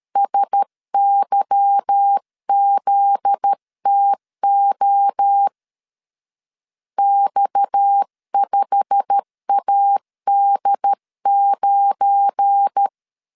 Morse code, heard as a keyed tone, about 13 wpm, 800 Hz. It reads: SYZTO X5AD9
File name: LEZIONI / 8 YJ45 / 8 YJ45 12wpm 2151